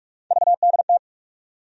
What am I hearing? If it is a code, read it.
Morse code, heard as a keyed tone, about 46 wpm, 700 Hz. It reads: VDT
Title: USA Call Sign Postfixes - 3 Letters 45wpm